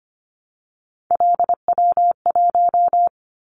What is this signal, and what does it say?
Morse code, heard as a keyed tone, about 25 wpm, 700 Hz. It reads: LW1
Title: Callsign Prefixes for Top 50 DXCC - 25wpm